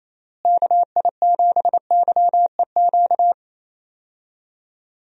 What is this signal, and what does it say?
Morse code, heard as a keyed tone, about 28 wpm, 700 Hz. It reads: KI7YEQ
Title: USA Call Signs - 6 Characters - Repeated 3x - 28wpm